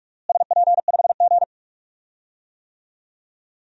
Morse code, heard as a keyed tone, about 47 wpm, 700 Hz. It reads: SP5G
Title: DXCC-100 Call Signs - 4 Characters - Repeated 3x - 45wpm